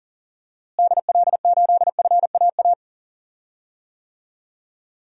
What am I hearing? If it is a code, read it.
Morse code, heard as a keyed tone, about 37 wpm, 700 Hz. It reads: DL8FAA
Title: DXCC-100 Call Signs - 6 Characters - Repeated 2x - 40wpm